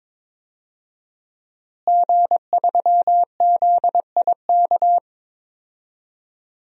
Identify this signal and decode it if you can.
Morse code, heard as a keyed tone, about 22 wpm, 700 Hz. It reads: G3ZIK
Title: DXCC-100 Call Signs - 5 Characters - Repeated 2x - 22wpm